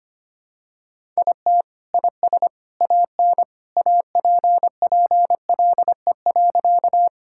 Morse code, heard as a keyed tone, about 25 wpm, 700 Hz. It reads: IT IS AN APPLE.
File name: practice / 25wpm / Sentences with 2-Word Phrases - 25wpm